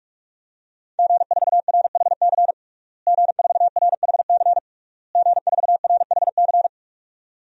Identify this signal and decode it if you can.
Morse code, heard as a keyed tone, about 46 wpm, 700 Hz. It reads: G4RHC G4RHC G4RHC
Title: DXCC-100 Call Signs - 5 Characters - Repeated 3x - 45wpm